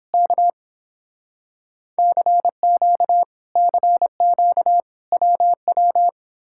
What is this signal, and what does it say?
Morse code, heard as a keyed tone, about 29 wpm, 700 Hz. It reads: K CQ CQ WW